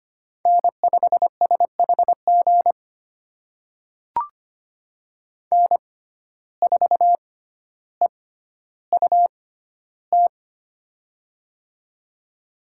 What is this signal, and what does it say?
Morse code, heard as a keyed tone, about 25 wpm, spaced as for 13 wpm, 700 Hz. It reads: N5SHG N4EUT
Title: USA Call Signs - 5 Characters - 6x Character Spacing - 25wpm